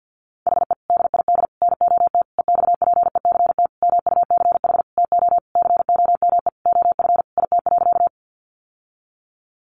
Morse code, heard as a keyed tone, about 50 wpm, 700 Hz. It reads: HE DID NOT EXPECT MUCH TO COME OF IT.
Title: Sentences from Top 1100 Words 50wpm